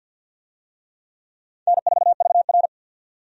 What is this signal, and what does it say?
Morse code, heard as a keyed tone, about 52 wpm, 700 Hz. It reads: N4VR
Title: USA Call Signs - 4 Characters - 50wpm